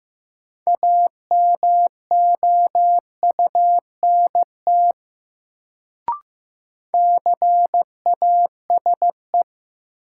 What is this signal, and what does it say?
Morse code, heard as a keyed tone, about 15 wpm, 700 Hz. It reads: AMOUNT CASE
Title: Sentences from Top 700 Words - New Word Drill - 15wpm